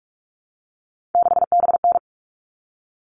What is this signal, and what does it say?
Morse code, heard as a keyed tone, about 44 wpm, 700 Hz. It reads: 6BN
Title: All Cut Numbers - 3-Digit 45wpm